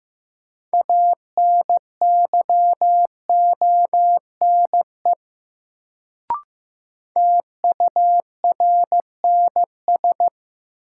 Morse code, heard as a keyed tone, about 15 wpm, 700 Hz. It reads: ANYONE TURNS
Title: Sets of 5 Words Encore - New Word Drill - 15wpm